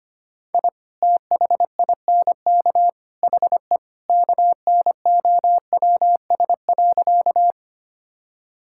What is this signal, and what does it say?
Morse code, heard as a keyed tone, about 25 wpm, 700 Hz. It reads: I THINK HE KNOWS.
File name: practice / 25wpm / Sentences with 3-Word Phrases - 25wpm